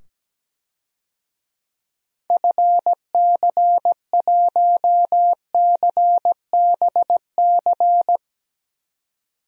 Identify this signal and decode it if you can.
Morse code, heard as a keyed tone, about 17 wpm, 700 Hz. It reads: FC1CBC
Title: DXCC-100 Call Signs - 6 Characters - 17wpm